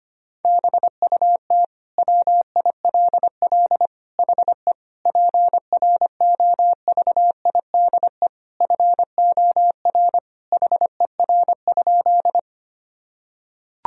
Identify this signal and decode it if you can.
Morse code, heard as a keyed tone, about 25 wpm, 700 Hz. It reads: BUT WILL HE PROVIDE FOR HER?
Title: Sentences from Top 1000 Words - Review Part 3 - 25wpm